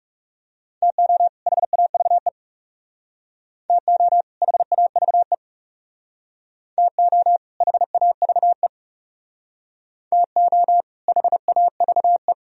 Morse code, heard as a keyed tone, about 39 wpm, 700 Hz. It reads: TO HAVE TO HAVE TO HAVE TO HAVE